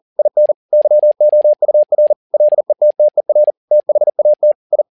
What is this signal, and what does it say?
Morse code, heard as a keyed tone, about 40 wpm, 600 Hz. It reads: IN YOUR LETTER THAT I
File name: learning center / tower trouble / tower touble 40wpm